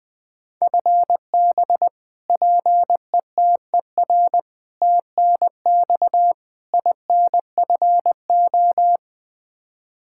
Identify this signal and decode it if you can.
Morse code, heard as a keyed tone, about 20 wpm, 700 Hz. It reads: FB PETER TNX INFO
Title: Practice QSO - Exchange 3 - More Info - 20wpm